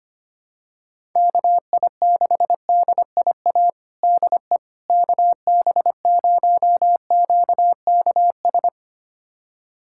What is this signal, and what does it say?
Morse code, heard as a keyed tone, about 25 wpm, 700 Hz. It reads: KI6DIA DE KB0QKS
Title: USA Callsign DE Callsign 25wpm